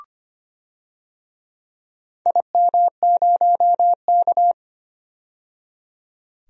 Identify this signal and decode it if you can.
Morse code, heard as a keyed tone, about 25 wpm, 700 Hz. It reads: IM0K
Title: DXCC-100 Call Signs - 4 Characters - 25wpm